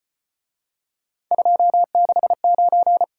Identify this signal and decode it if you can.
Morse code, heard as a keyed tone, about 33 wpm, 700 Hz. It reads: 269